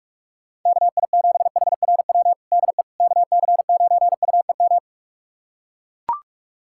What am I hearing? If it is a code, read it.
Morse code, heard as a keyed tone, about 42 wpm, 700 Hz. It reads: KI7HRW DE KC9UEM
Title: USA Callsign DE Callsign - 6 Characters - 45wpm